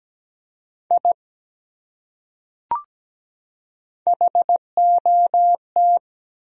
Morse code, heard as a keyed tone, about 17 wpm, 700 Hz. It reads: I HOT
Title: Top 100 Words 17wpm